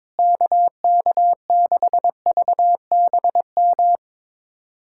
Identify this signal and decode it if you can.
Morse code, heard as a keyed tone, about 22 wpm, 700 Hz. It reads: KK6VBM
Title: USA Callsign DE Callsign - 6 Characters - 22wpm